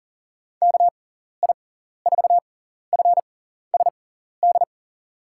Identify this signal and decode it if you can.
Morse code, heard as a keyed tone, about 40 wpm, spaced as for 14 wpm, 700 Hz. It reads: KI4FSD
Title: USA Call Signs - 6x Character Spacing - 40wpm